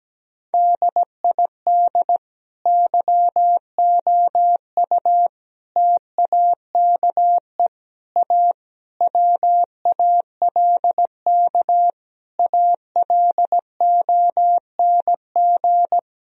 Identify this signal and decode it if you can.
Morse code, heard as a keyed tone, about 17 wpm, 700 Hz. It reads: DID YOU TAKE A WALK ALONG
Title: Sentences from Top 1000 Words - Review Part 4 - 17wpm